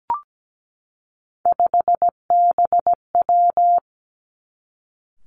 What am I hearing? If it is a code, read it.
Morse code, heard as a keyed tone, about 17 wpm, 700 Hz. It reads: 5BW
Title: Three Challenging Letter-Number Combinations 17wpm